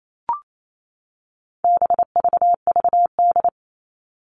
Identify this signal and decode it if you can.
Morse code, heard as a keyed tone, about 28 wpm, 700 Hz. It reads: BVVD
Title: All Cut Numbers - 4-Digit 28wpm